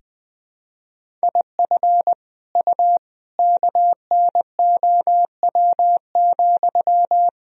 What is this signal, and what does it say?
Morse code, heard as a keyed tone, about 20 wpm, 700 Hz. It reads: IF U KNOW,
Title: Sentences from Top 300 Words and Abbreviations 20wpm